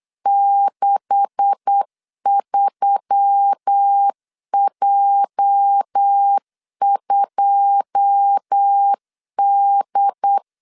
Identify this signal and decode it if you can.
Morse code, heard as a keyed tone, about 8 wpm, 800 Hz. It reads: 63J2D
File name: LEZIONI / 10 SIGNS / 10D COMA 8wpm 3749